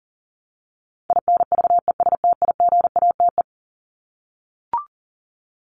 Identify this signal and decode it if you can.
Morse code, heard as a keyed tone, about 40 wpm, 700 Hz. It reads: INVESTIGATE
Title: Top 1751-2000 Words 40wpm